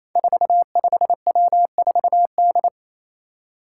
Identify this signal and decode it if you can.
Morse code, heard as a keyed tone, about 27 wpm, 700 Hz. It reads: 45W4D
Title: Five Challenging Letter-Number Combinations 28wpm